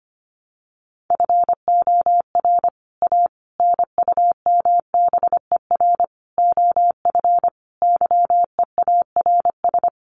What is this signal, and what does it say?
Morse code, heard as a keyed tone, about 25 wpm, 700 Hz. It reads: FOR A NUMBER OF YEARS
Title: Sets of 5 Words Encore 25wpm